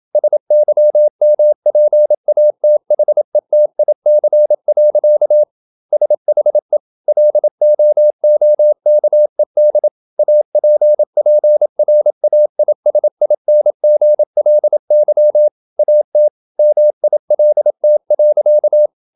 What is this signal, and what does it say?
Morse code, heard as a keyed tone, about 27 wpm, 600 Hz. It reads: SYMPATHETIC. SHE LOOKED APPRAISINGLY AT MILT.
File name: Never 0027wpm